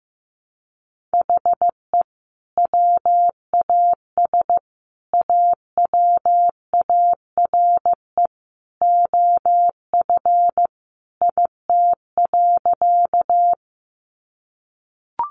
Morse code, heard as a keyed tone, about 15 wpm, 700 Hz. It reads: HE WAS AWARE OF IT.
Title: Sentences from Top 1200 Words 15wpm